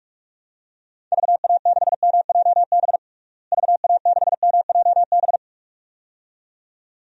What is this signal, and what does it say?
Morse code, heard as a keyed tone, about 46 wpm, 700 Hz. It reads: VA6MJB VA6MJB
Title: DXCC-100 Call Signs - 6 Characters - Repeated 2x - 45wpm